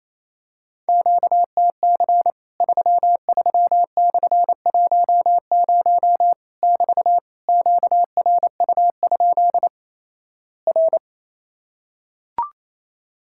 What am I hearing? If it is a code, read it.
Morse code, heard as a keyed tone, about 28 wpm, 700 Hz. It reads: QTC 33/10 <BT> QRU? R
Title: Worked All Europe DX Contest - QTC Exchange - 28wpm